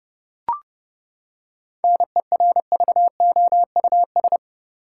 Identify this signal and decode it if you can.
Morse code, heard as a keyed tone, about 32 wpm, 700 Hz. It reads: NERVOUS
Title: Seven Letter Words - 30wpm